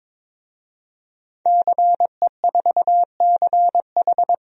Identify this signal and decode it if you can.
Morse code, heard as a keyed tone, about 22 wpm, 700 Hz. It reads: CE4CH
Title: DXCC-100 Call Signs - 5 Characters - Repeated 2x - 22wpm